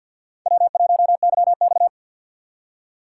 Morse code, heard as a keyed tone, about 56 wpm, 690 Hz. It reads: W1YX
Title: Speed Racing - USA 1x2 Call Signs 50wpm